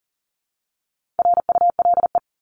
Morse code, heard as a keyed tone, about 40 wpm, 700 Hz. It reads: RULE
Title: ES:S Suffix - Top 100 Words with Base Words 40wpm